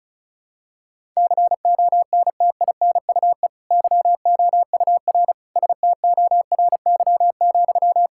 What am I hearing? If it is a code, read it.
Morse code, heard as a keyed tone, about 34 wpm, 700 Hz. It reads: CONTINUE YOUR STORY,
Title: Sentences from Top 1000 Words - Review Part 6 - 35wpm